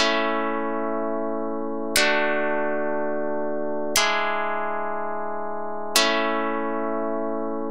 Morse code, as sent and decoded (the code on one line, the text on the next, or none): none